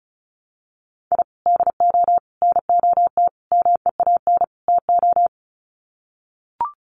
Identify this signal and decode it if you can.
Morse code, heard as a keyed tone, about 34 wpm, 700 Hz. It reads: I DO NOT MEAN TO
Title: Sets of 5 Words 35wpm